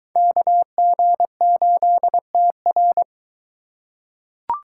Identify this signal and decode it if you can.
Morse code, heard as a keyed tone, about 23 wpm, 700 Hz. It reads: KG8TR